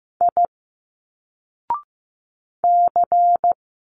Morse code, heard as a keyed tone, about 15 wpm, 700 Hz. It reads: I C
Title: Single Letters 15wpm